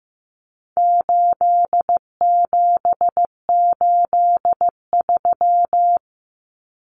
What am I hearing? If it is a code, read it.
Morse code, heard as a keyed tone, about 15 wpm, 700 Hz. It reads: 8783
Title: Numbers - 4-Digit 15wpm